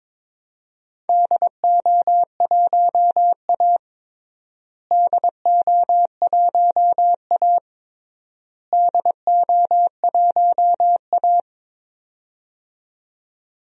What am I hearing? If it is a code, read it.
Morse code, heard as a keyed tone, about 22 wpm, 700 Hz. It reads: DO1A DO1A DO1A
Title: DXCC-100 Call Signs - 4 Characters - Repeated 3x - 22wpm